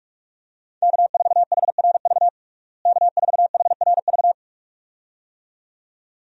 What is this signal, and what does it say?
Morse code, heard as a keyed tone, about 46 wpm, 700 Hz. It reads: K4HRV K4HRV
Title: USA Call Signs - 5 Characters - Repeated 2x - 45wpm